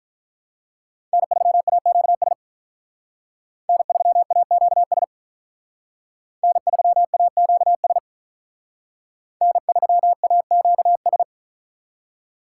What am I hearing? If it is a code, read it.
Morse code, heard as a keyed tone, about 45 wpm, 700 Hz. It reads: N3AQS N3AQS N3AQS N3AQS